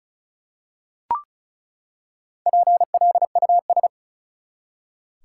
Morse code, heard as a keyed tone, about 34 wpm, 700 Hz. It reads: PLUS
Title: Four Letter Words - 35wpm